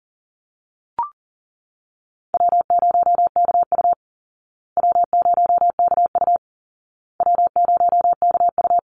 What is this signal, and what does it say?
Morse code, heard as a keyed tone, about 40 wpm, 700 Hz. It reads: W0KU W0KU W0KU
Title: Easy Call Signs Repeated 3x - 40wpm